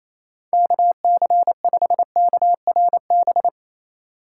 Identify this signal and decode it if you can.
Morse code, heard as a keyed tone, about 28 wpm, 700 Hz. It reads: KC5KRB
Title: USA Call Signs - 3x Character Spacing - 28wpm